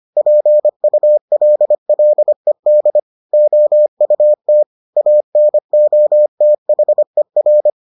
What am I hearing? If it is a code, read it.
Morse code, heard as a keyed tone, about 25 wpm, 600 Hz. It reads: PULLED OUT ANOTHER